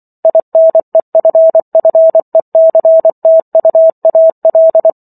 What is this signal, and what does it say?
Morse code, heard as a keyed tone, about 24 wpm, 650 Hz. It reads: INEFFECTUAL